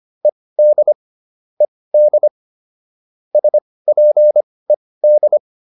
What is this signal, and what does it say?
Morse code, heard as a keyed tone, about 25 wpm, spaced as for 18 wpm, 600 Hz. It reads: ED ED SPED